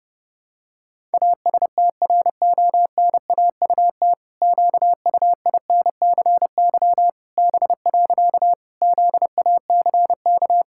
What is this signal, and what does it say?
Morse code, heard as a keyed tone, about 32 wpm, 700 Hz. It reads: ASTRONAUT QUINCY B. ZACK
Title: Pangram Challenge v2 - 30wpm